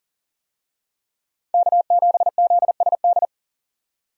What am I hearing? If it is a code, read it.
Morse code, heard as a keyed tone, about 40 wpm, 700 Hz. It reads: K7ZSD
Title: USA Call Signs - 5 Characters - 40wpm